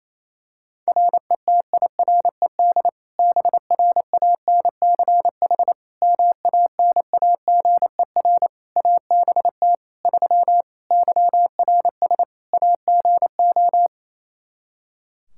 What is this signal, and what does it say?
Morse code, heard as a keyed tone, about 28 wpm, 700 Hz. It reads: RETIRED BRANCH MANAGER ABT 3 YRS AGO
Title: Practice QSO - Exchange 3 - More Info - 28wpm